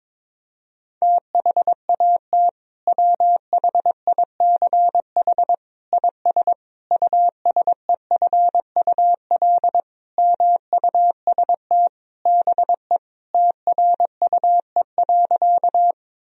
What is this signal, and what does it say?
Morse code, heard as a keyed tone, about 21 wpm, 700 Hz. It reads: THAT WHICH IS USEFUL MUST BE TRUE.
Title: Sentences from Top 1000 Words - Review Part 4 - 22wpm